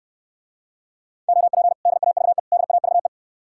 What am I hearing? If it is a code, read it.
Morse code, heard as a keyed tone, about 68 wpm, 700 Hz. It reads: OP DAVE DAVE